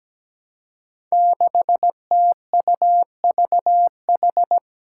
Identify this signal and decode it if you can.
Morse code, heard as a keyed tone, about 17 wpm, 700 Hz. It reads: 6TUVH
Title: Five Challenging Letter-Number Combinations 17wpm